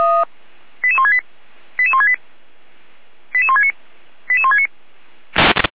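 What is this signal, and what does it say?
Five-tone selective call (ZVEI), heard as quick runs of five tones, tones of about 70 ms. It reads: 88168 88168 88169 88169
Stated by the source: ZVEI five-tone selective call received on an SDR.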